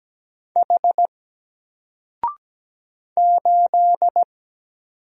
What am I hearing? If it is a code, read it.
Morse code, heard as a keyed tone, about 17 wpm, 700 Hz. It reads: H 8
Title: Single Letter-Number Alt 17wpm